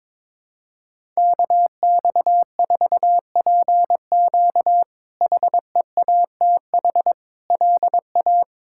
Morse code, heard as a keyed tone, about 22 wpm, 700 Hz. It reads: KX4PQ HEATH LA
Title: K1USN SST QSO Practice - 22wpm